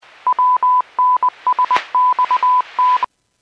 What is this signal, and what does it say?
Morse code, heard as a keyed tone, about 20 wpm, 1000 Hz. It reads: WNSXT